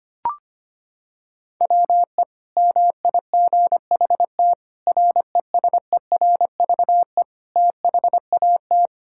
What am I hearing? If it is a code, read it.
Morse code, heard as a keyed tone, about 25 wpm, 700 Hz. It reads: WE MIGHT RESERVE THAT
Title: Sentences from Top 1000 Words - Review Part 6 - 25wpm